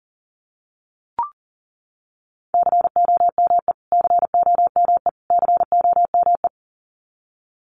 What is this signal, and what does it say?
Morse code, heard as a keyed tone, about 40 wpm, 700 Hz. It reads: COME COME COME
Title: Top 500 Words Repeated 40wpm